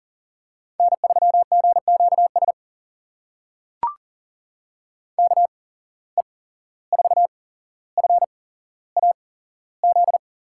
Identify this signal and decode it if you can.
Morse code, heard as a keyed tone, about 40 wpm, spaced as for 16 wpm, 700 Hz. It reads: N3GQS KE4FAZ
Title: USA Call Signs - 8x Character Spacing - 40wpm